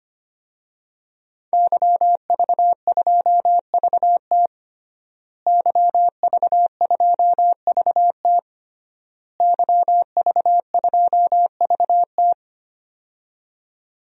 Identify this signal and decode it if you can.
Morse code, heard as a keyed tone, about 25 wpm, 700 Hz. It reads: YV2VT YV2VT YV2VT
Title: DXCC-100 Call Signs - 5 Characters - Repeated 3x - 25wpm